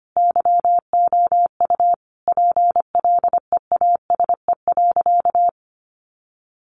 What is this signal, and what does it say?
Morse code, heard as a keyed tone, about 25 wpm, 700 Hz. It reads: YOU PLEASE.
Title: Sentences from Top 1500 Words 25wpm